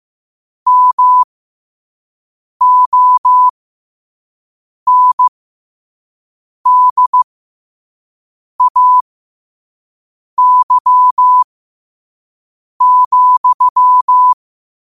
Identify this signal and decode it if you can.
Morse code, heard as a keyed tone, about 15 wpm, spaced as for 5 wpm, 1000 Hz. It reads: MONDAY,
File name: qotdfiles / 5wpm